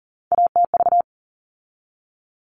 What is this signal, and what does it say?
Morse code, heard as a keyed tone, about 40 wpm, 700 Hz. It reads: ATV
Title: All Cut Numbers - 3-Digit 40wpm